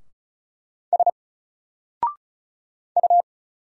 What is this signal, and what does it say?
Morse code, heard as a keyed tone, about 36 wpm, 700 Hz. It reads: S U
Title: Single Letter-Number Alt 35wpm